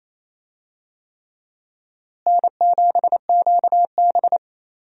Morse code, heard as a keyed tone, about 27 wpm, 700 Hz. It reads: N7QB
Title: USA Call Signs - 4 Characters - Repeated 3x - 28wpm